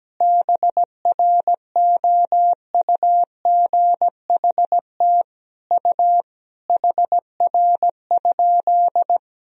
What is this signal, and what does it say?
Morse code, heard as a keyed tone, about 17 wpm, 700 Hz. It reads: BROUGHT U HR?
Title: Sentences from Top 500 Words and Abbreviations 17wpm